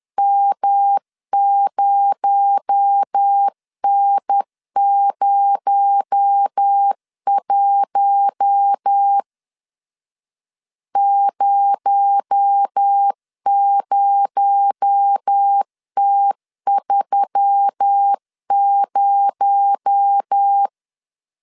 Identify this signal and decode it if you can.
Morse code, heard as a keyed tone, about 10 wpm, 800 Hz. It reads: M0N01 00T30
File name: LEZIONI / 6 PKX0 / 6 PKX0 10wpm 2707